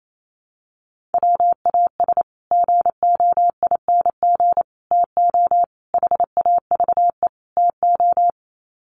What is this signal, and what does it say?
Morse code, heard as a keyed tone, about 28 wpm, 700 Hz. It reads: WAS GOING TO HAVE TO
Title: Sets of 5 Words Encore 28wpm